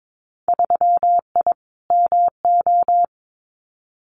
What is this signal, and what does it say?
Morse code, heard as a keyed tone, about 22 wpm, 700 Hz. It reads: 3I MO
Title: Winter Field Day Exchanges 22wpm